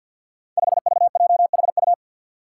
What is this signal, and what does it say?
Morse code, heard as a keyed tone, about 50 wpm, 700 Hz. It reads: 5VJHU
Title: Five Challenging Letter-Number Combinations 50wpm